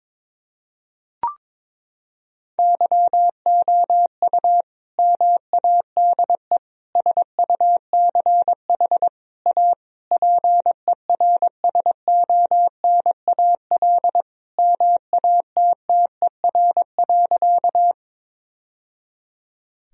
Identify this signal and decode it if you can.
Morse code, heard as a keyed tone, about 22 wpm, 700 Hz. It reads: YOU MADE SUCH A PERSONAL MATTER.
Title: Sentences from Top 1000 Words - Review Part 4 - 22wpm